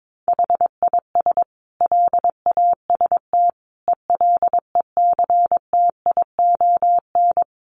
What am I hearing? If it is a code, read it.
Morse code, heard as a keyed tone, about 22 wpm, 700 Hz. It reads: HIS LAST ELECTION
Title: Sentences from Top 1500 Words 22wpm